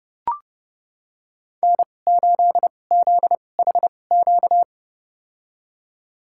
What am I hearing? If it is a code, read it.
Morse code, heard as a keyed tone, about 30 wpm, spaced as for 22 wpm, 700 Hz. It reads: N8ZHQ
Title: USA Call Signs - 5 Characters - 2x Character Spacing - 30wpm